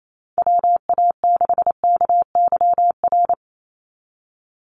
Morse code, heard as a keyed tone, about 28 wpm, 700 Hz. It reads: WA6KYR
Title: NA SOTA Chaser Callsigns 28wpm